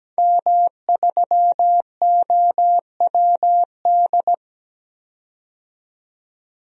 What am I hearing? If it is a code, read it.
Morse code, heard as a keyed tone, about 17 wpm, 700 Hz. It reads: M3OWD
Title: DXCC-100 Call Signs - 5 Characters - 17wpm